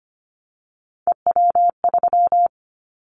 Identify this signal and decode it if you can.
Morse code, heard as a keyed tone, about 25 wpm, 700 Hz. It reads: EW3
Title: Callsign Prefixes for Top 50 DXCC - 25wpm